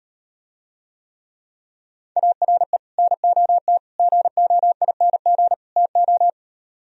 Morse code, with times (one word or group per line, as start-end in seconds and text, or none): 2.2-2.8 ARE
3.0-3.8 NOT
4.0-5.5 GOING
5.8-6.3 TO